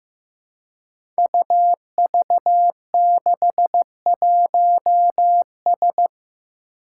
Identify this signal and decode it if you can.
Morse code, heard as a keyed tone, about 15 wpm, 700 Hz. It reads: UV61S
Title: Five Challenging Letter-Number Combinations 15wpm